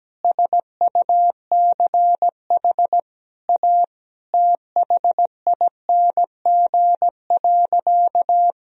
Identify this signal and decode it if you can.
Morse code, heard as a keyed tone, about 17 wpm, 700 Hz. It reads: SUCH A THING.
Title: Sentences from Top 200 Words and Abbreviations 17wpm